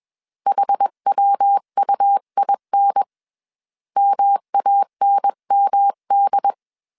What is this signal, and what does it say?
Morse code, heard as a keyed tone, about 21 wpm, 800 Hz. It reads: HWUIN MANMD
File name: CW / DET / 1 3 1500C S B 20wpm 1754